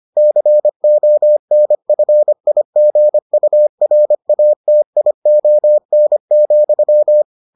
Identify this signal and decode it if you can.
Morse code, heard as a keyed tone, about 25 wpm, 600 Hz. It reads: CONFIGURATION,